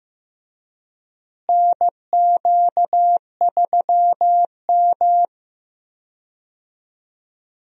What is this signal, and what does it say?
Morse code, heard as a keyed tone, about 16 wpm, 700 Hz. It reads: NQ3M
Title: USA Call Signs - 4 Characters - Repeated 2x - 15wpm